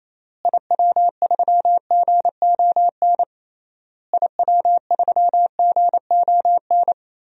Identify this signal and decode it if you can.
Morse code, heard as a keyed tone, about 28 wpm, 700 Hz. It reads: IW3GON IW3GON
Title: DXCC-100 Call Signs - 6 Characters - Repeated 3x - 28wpm